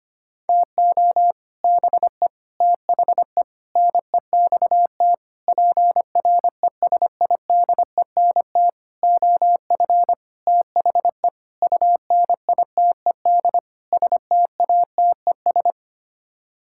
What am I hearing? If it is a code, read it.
Morse code, heard as a keyed tone, about 25 wpm, 700 Hz. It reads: TO BE THE NEXT PRESIDENT OF THE UNITED STATES
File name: Sets of 9 Words - 25wpm